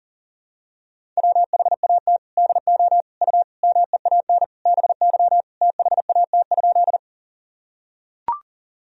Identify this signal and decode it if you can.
Morse code, heard as a keyed tone, about 40 wpm, 700 Hz. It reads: WHAT DO U MEAN BY THAT?
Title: Sentences from Top 200 Words and Abbreviations 40wpm